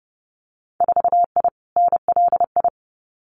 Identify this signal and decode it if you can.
Morse code, heard as a keyed tone, about 30 wpm, 700 Hz. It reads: 4I NLI
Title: Winter Field Day Exchanges Encore 30wpm